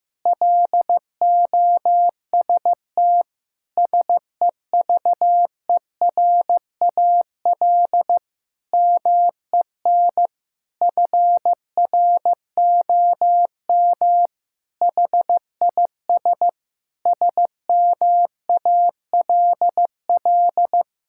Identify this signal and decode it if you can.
Morse code, heard as a keyed tone, about 15 wpm, 700 Hz. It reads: LOST SEVERAL MEN FROM HIS SMALL
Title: Sentences from Top 1600 Words 15wpm